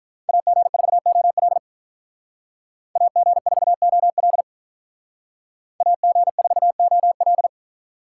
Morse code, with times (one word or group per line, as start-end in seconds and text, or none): 0.3-1.6 AG4OL
3.0-4.4 AG4OL
5.8-7.5 AG4OL